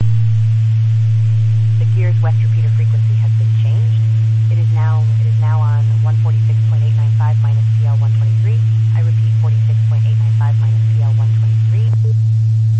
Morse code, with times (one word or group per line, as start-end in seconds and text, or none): none